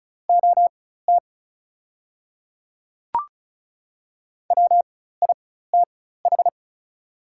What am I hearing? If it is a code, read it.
Morse code, heard as a keyed tone, about 34 wpm, spaced as for 16 wpm, 690 Hz. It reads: OT WITH
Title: 4x Character Spacing - Top 500 Words - 35wpm